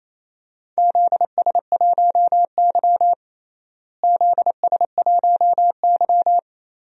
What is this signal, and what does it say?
Morse code, heard as a keyed tone, about 28 wpm, 700 Hz. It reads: ZS1Y ZS1Y
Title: DXCC-100 Call Signs - 4 Characters - Repeated 2x - 28wpm